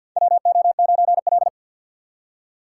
Morse code, heard as a keyed tone, about 53 wpm, 700 Hz. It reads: WO9L
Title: USA Call Signs - 4 Characters - 2x Character Spacing - 50wpm